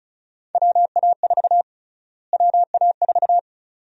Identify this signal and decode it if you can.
Morse code, heard as a keyed tone, about 34 wpm, 700 Hz. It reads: WA4 WA4
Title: USA Call Sign Prefixes - Repeated 2x - 35wpm